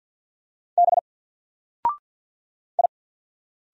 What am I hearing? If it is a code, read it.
Morse code, heard as a keyed tone, about 48 wpm, 700 Hz. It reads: B I
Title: Single Letter-Number 50wpm